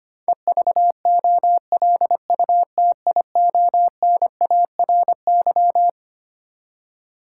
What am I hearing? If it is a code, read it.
Morse code, heard as a keyed tone, about 25 wpm, 700 Hz. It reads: EVOLUTIONARY
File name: Twelve Letter Words - 25wpm